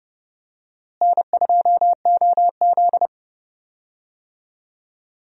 Morse code, heard as a keyed tone, about 30 wpm, 700 Hz. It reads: N2OZ
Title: USA Call Signs - 4 Characters - Repeated 2x - 30wpm